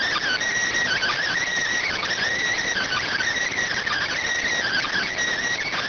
Emergency Alert System (SAME) tones (header bursts), heard as harsh buzzing bursts. It is heard from 0.8 to 5.1 s.